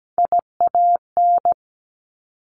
Morse code, heard as a keyed tone, about 17 wpm, 700 Hz. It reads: IAN
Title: Top Three Letter Combinations in Words 17wpm